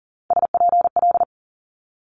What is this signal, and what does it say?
Morse code, heard as a keyed tone, about 38 wpm, 690 Hz. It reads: SPL
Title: Multi-letter Phonemes 40wpm